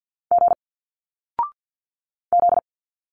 Morse code, heard as a keyed tone, about 50 wpm, 700 Hz. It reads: G Z
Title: Single Letters 50wpm